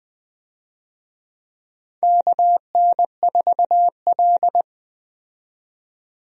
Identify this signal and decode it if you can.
Morse code, heard as a keyed tone, about 20 wpm, 700 Hz. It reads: KN4L
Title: USA Call Signs - 4 Characters - 20wpm